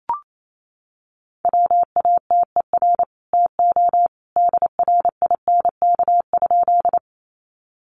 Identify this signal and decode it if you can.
Morse code, heard as a keyed tone, about 28 wpm, 700 Hz. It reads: WATER TO DRINK?
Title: Sentences from Top 1500 Words 28wpm